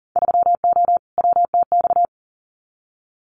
Morse code, heard as a keyed tone, about 41 wpm, 700 Hz. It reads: 3O WTX
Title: Winter Field Day Exchanges Encore 40wpm